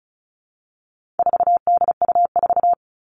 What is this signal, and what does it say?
Morse code, heard as a keyed tone, about 34 wpm, 700 Hz. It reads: 4DU4
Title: All Cut Numbers - 4-Digit 35wpm